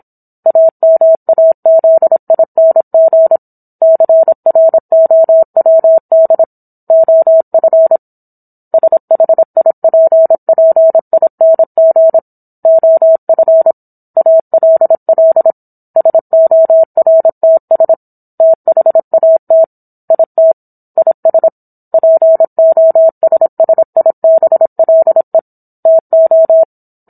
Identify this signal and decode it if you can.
Morse code, heard as a keyed tone, about 27 wpm, 650 Hz. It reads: AMAZING CROWD OF SHIPPING OF ALL SORTS THAT IT IS POSSIBLE TO